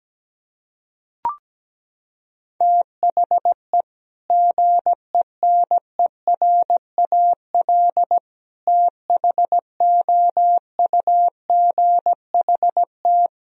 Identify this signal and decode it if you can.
Morse code, heard as a keyed tone, about 17 wpm, 700 Hz. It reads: THE GENERAL THOUGHT
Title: Sentences from Top 1000 Words - Review Part 3 - 17wpm